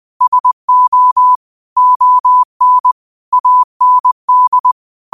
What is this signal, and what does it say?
Morse code, heard as a keyed tone, about 20 wpm, 1000 Hz. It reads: SO ON AND
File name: qotdfiles / 20wpm